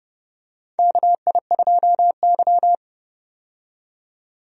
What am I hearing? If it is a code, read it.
Morse code, heard as a keyed tone, about 30 wpm, 700 Hz. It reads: KI2Y
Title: USA Call Signs - 4 Characters - 30wpm